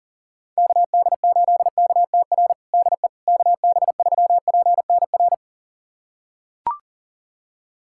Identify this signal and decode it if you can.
Morse code, heard as a keyed tone, about 40 wpm, 700 Hz. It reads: KD8KTR DE KB3PNR
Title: USA Callsign DE Callsign - 6 Characters - 40wpm